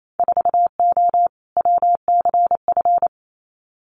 Morse code, heard as a keyed tone, about 28 wpm, 700 Hz. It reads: 4O WCF
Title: Winter Field Day Exchanges Encore 28wpm